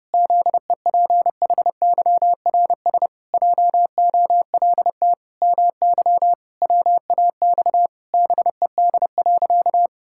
Morse code, heard as a keyed tone, about 28 wpm, 700 Hz. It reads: ZEPHYRS JOLT MY WAX BED.
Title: Pangram Challenge v2 - 30wpm